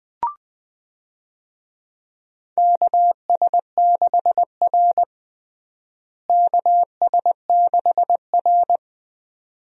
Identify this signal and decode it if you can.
Morse code, heard as a keyed tone, about 21 wpm, 700 Hz. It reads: KS6R KS6R
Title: USA Call Signs - 4 Characters - Repeated 2x - 20wpm